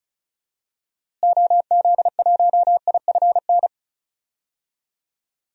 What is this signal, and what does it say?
Morse code, heard as a keyed tone, about 36 wpm, 700 Hz. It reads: OZ1IFN
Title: DXCC-100 Call Signs - 6 Characters - Repeated 3x - 35wpm